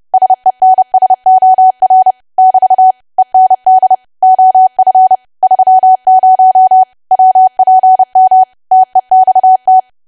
Morse code, heard as a keyed tone, about 30 wpm, 750 Hz. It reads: SENSOR <BT> END OF 30 WPM TEXT
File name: cwa / W1AW-Level3 / 30WPM